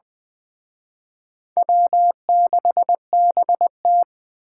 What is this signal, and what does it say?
Morse code, heard as a keyed tone, about 20 wpm, 700 Hz. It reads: W6BT
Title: Speed Racing - USA 1x2 Call Signs 17wpm